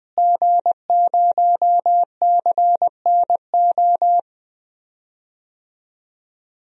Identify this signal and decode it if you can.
Morse code, heard as a keyed tone, about 21 wpm, 700 Hz. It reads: G0CNO
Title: DXCC-100 Call Signs - 5 Characters - Repeated 3x - 20wpm